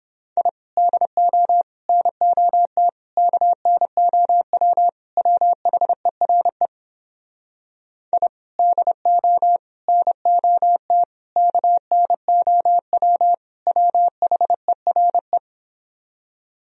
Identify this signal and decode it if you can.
Morse code, heard as a keyed tone, about 29 wpm, 700 Hz. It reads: I DO NOT KNOW WHERE I DO NOT KNOW WHERE